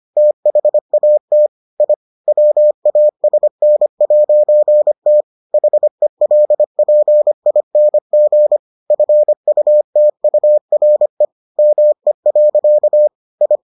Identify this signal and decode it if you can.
Morse code, heard as a keyed tone, about 25 wpm, 600 Hz. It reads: THAT I WASN'T HELPING FUTURE ME. I